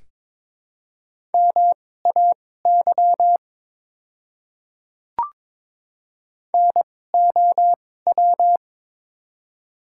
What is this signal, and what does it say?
Morse code, heard as a keyed tone, about 21 wpm, spaced as for 16 wpm, 700 Hz. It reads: MAY NOW
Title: 2x Character Spacing - Top 100 Words - 22wpm